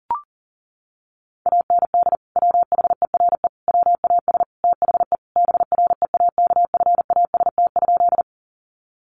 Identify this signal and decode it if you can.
Morse code, heard as a keyed tone, about 40 wpm, 700 Hz. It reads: AND WHERE WAS THE BREAKFAST?
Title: Sentences from Top 1500 Words 40wpm